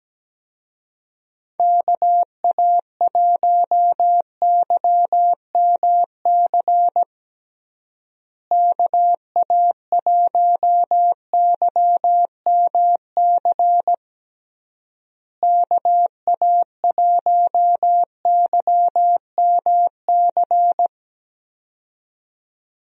Morse code, heard as a keyed tone, about 17 wpm, 700 Hz. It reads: KA1YMC KA1YMC KA1YMC
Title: USA Call Signs - 6 Characters - Repeated 3x - 17wpm